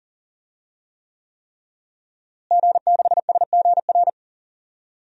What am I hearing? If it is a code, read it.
Morse code, heard as a keyed tone, about 40 wpm, 700 Hz. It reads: G6SGR
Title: DXCC-100 Call Signs - 5 Characters - Repeated 3x - 40wpm